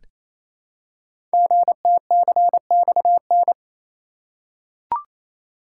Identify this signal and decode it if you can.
Morse code, heard as a keyed tone, about 27 wpm, 700 Hz. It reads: GTCXN
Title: Code Groups - 5 Letters - 28wpm